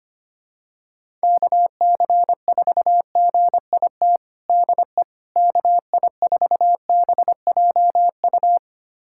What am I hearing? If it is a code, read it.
Morse code, heard as a keyed tone, about 25 wpm, 700 Hz. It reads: KC4GIT DE KI4BJU
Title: USA Callsign DE Callsign 25wpm